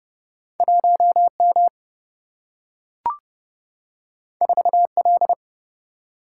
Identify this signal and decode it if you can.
Morse code, heard as a keyed tone, about 30 wpm, 700 Hz. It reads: 1M 4L
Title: Single-Letter Single-Number 30wpm